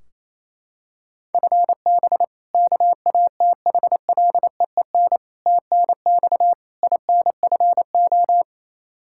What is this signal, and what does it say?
Morse code, heard as a keyed tone, about 28 wpm, 700 Hz. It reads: FB KATHLEEN TNX INFO
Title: Practice QSO - Exchange 3 - More Info - 28wpm